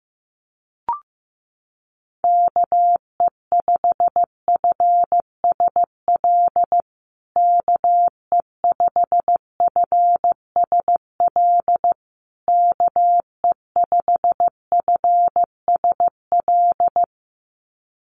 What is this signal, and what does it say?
Morse code, heard as a keyed tone, about 15 wpm, 700 Hz. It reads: KE5FSL KE5FSL KE5FSL
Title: USA Call Signs Repeated 3x 15wpm